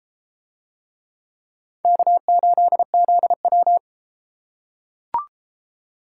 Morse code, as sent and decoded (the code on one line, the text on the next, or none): -.- ---.. --.. .--
K8ZW